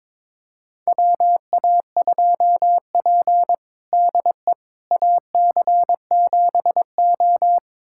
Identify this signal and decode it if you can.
Morse code, heard as a keyed tone, about 22 wpm, 700 Hz. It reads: WA2P DE AC7O
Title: USA Callsign DE Callsign - 4 Characters - 22wpm